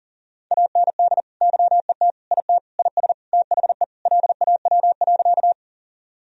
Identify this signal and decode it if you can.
Morse code, heard as a keyed tone, about 40 wpm, 700 Hz. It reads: AND YET IT IS THE LAW.
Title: Sentences from Top 1000 Words - Review Part 3 - 40wpm